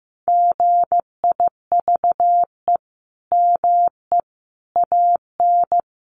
Morse code, heard as a keyed tone, about 15 wpm, 700 Hz. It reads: GIVE ME AN
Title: Sentences from Top 1300 Words 15wpm